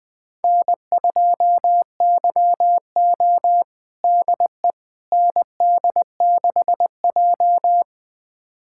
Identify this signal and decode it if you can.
Morse code, heard as a keyed tone, about 20 wpm, 700 Hz. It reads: N2YO DE ND6J
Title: USA Callsign DE Callsign - 4 Characters - 20wpm